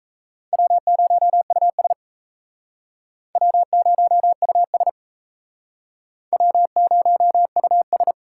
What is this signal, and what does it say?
Morse code, heard as a keyed tone, about 37 wpm, 700 Hz. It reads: W0US W0US W0US